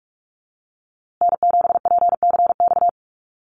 Morse code, heard as a keyed tone, about 44 wpm, 700 Hz. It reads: N7PCX
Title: USA Call Signs Repeated 3x 45wpm